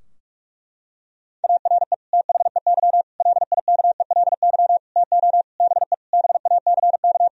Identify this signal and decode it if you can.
Morse code, heard as a keyed tone, about 46 wpm, 700 Hz. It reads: ARE THEY LIKELY TO BE BACK
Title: Sentences from Top 1000 Words - Review Part 4 - 45wpm